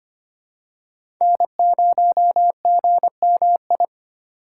Morse code, heard as a keyed tone, about 25 wpm, 700 Hz. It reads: N0GMI
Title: USA Call Signs - 5 Characters - 4x Character Spacing - 25wpm